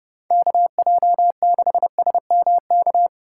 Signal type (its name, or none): Morse code